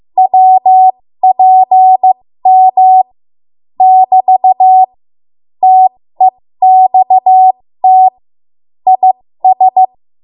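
Morse code, heard as a keyed tone, about 15 wpm, spaced as for 13 wpm, 750 Hz. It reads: WPM <BT> TEXT IS
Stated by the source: 13 wpm Morse code practice archive transmission.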